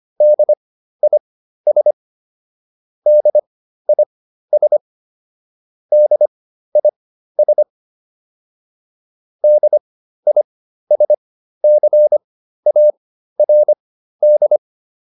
Morse code, heard as a keyed tone, about 25 wpm, spaced as for 13 wpm, 600 Hz. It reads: DIS DIS DIS DISCARD
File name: DIS7 13